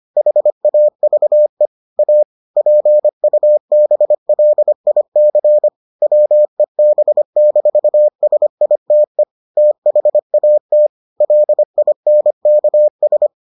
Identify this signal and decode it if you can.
Morse code, heard as a keyed tone, about 25 wpm, 600 Hz. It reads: HAVE A PUBLIC WEB-SITE THAT LINKS